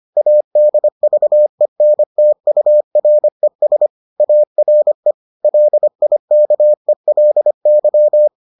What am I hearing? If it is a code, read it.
Morse code, heard as a keyed tone, about 25 wpm, 600 Hz. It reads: ADVENTURES ARE LIKELY